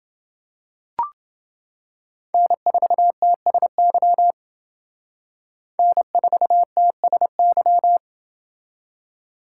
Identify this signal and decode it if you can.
Morse code, heard as a keyed tone, about 28 wpm, 700 Hz. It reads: N4TSY N4TSY